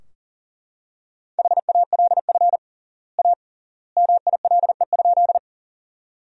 Morse code, heard as a keyed tone, about 40 wpm, 700 Hz. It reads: HALF A MILE?
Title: 3x Word-Spacing - Sentences from Top 400 Words - 40wpm